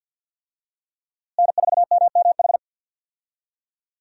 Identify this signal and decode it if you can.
Morse code, heard as a keyed tone, about 47 wpm, 700 Hz. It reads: N4MMH
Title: USA Call Signs - 5 Characters - Repeated 3x - 50wpm